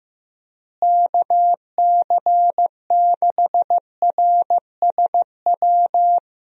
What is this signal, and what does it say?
Morse code, heard as a keyed tone, about 15 wpm, 700 Hz. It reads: KC6RSW
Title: USA Call Signs - 6 Characters - Repeated 3x - 15wpm